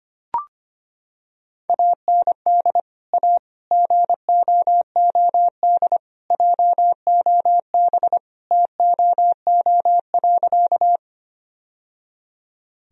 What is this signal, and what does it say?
Morse code, heard as a keyed tone, about 25 wpm, 700 Hz. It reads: AND A GOOD JOB TOO.
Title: Sentences from Top 1000 Words - Review Part 3 - 25wpm